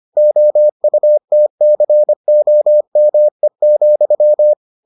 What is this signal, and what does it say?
Morse code, heard as a keyed tone, about 25 wpm, 600 Hz. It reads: OUTCOME,